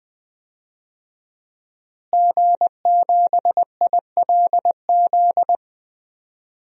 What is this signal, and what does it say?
Morse code, heard as a keyed tone, about 20 wpm, 700 Hz. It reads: G7ILZ
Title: DXCC-100 Call Signs - 5 Characters - Repeated 3x - 20wpm